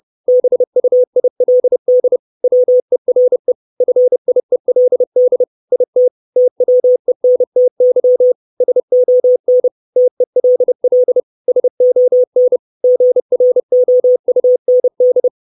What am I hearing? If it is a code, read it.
Morse code, heard as a keyed tone, about 30 wpm, 500 Hz. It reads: BUILD WERE FIELD IT TWENTY SON TELL SON GROUND
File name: 632 Words 30WPM